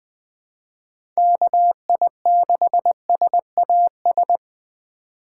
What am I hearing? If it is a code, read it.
Morse code, heard as a keyed tone, about 20 wpm, 700 Hz. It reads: KI6SAS
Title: USA Call Signs - 3x Character Spacing - 20wpm